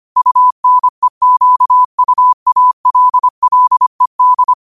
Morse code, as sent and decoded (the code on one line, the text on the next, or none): ..- -. --.- ..- .- .-.. .-.. -..
UNEQUALLED